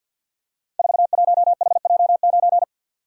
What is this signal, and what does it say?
Morse code, heard as a keyed tone, about 47 wpm, 700 Hz. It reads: 41HJ9
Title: Code Groups - 5 Letter:Numbers - 50wpm